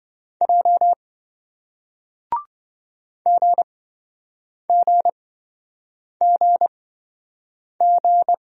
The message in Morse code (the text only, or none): J GGGG